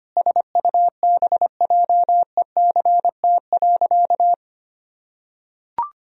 Morse code, heard as a keyed tone, about 25 wpm, 700 Hz. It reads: SUBJECT.
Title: Sentences from Top 1000 Words - Review Part 4 - 25wpm